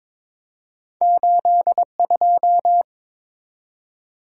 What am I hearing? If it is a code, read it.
Morse code, heard as a keyed tone, about 22 wpm, 700 Hz. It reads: 82
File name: practice / 22wpm / Two Letter-Number Combinations - 22wpm